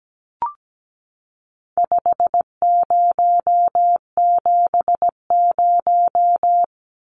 Morse code, heard as a keyed tone, about 17 wpm, 700 Hz. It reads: 5070
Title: Numbers - 4-Digit 17wpm